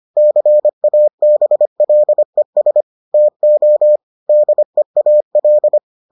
Morse code, heard as a keyed tone, about 25 wpm, 600 Hz. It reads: CABLES TO DEAL